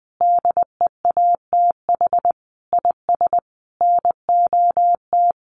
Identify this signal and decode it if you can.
Morse code, heard as a keyed tone, about 20 wpm, 700 Hz. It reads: DEATH IS NOT
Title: Sentences from Top 1100 Words 20wpm